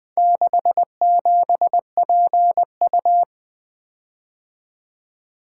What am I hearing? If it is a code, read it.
Morse code, heard as a keyed tone, about 23 wpm, 700 Hz. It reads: 67PU
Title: Speed-Racing - Four Letter-Number Combinations - 20wpm